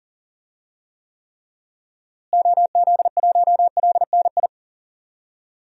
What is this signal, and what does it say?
Morse code, heard as a keyed tone, about 40 wpm, 700 Hz. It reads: OZ1LNI
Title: DXCC-100 Call Signs - 6 Characters - Repeated 3x - 40wpm